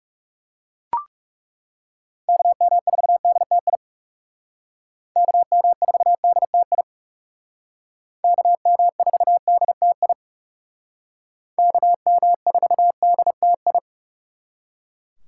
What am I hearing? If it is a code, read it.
Morse code, heard as a keyed tone, about 39 wpm, 700 Hz. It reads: KM4DTI KM4DTI KM4DTI KM4DTI